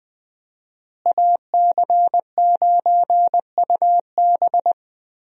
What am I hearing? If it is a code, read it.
Morse code, heard as a keyed tone, about 20 wpm, 700 Hz. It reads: AC9UB
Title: USA Call Signs - 5 Characters - Repeated 2x - 20wpm